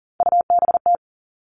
Morse code, heard as a keyed tone, about 40 wpm, 700 Hz. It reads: UBT
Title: All Cut Numbers - 3-Digit 40wpm